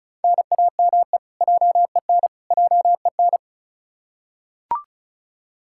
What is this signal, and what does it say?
Morse code, heard as a keyed tone, about 34 wpm, 700 Hz. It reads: NAME JEN JEN
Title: K1USN SST QSO Practice 35wpm